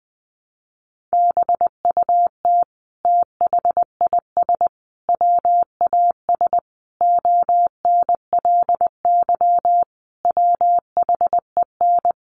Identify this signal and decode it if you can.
Morse code, heard as a keyed tone, about 20 wpm, 700 Hz. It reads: BUT THIS WAS ONLY WHEN
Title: Sentences from Top 1300 Words 20wpm